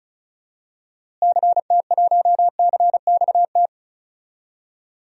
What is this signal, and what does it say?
Morse code, heard as a keyed tone, about 36 wpm, 700 Hz. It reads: CT1CXT
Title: DXCC-100 Call Signs - 6 Characters - Repeated 3x - 35wpm